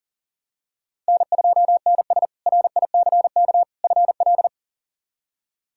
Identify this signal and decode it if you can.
Morse code, heard as a keyed tone, about 40 wpm, 700 Hz. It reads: N2NS RICK FL
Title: K1USN SST QSO Practice - 40wpm